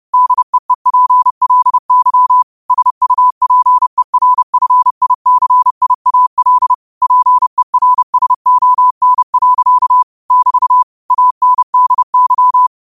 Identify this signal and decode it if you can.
Morse code, heard as a keyed tone, about 30 wpm, 1000 Hz. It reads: DEEPLY SUPERFICIAL PERSON. <BT> ANDY
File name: qotdfiles / 30wpm